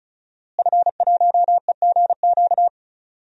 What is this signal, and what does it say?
Morse code, heard as a keyed tone, about 36 wpm, 700 Hz. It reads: F1EGQ
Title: DXCC-100 Call Signs - 5 Characters - Repeated 2x - 35wpm